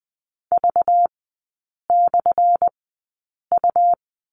Morse code, heard as a keyed tone, about 20 wpm, spaced as for 8 wpm, 690 Hz. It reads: V/U
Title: 212 Lesson 34e - Number 8 - Three Characters 20wpm